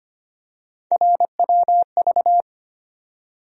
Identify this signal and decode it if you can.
Morse code, heard as a keyed tone, about 25 wpm, 700 Hz. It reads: RWV
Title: USA Call Sign Postfixes - 3 Letters 25wpm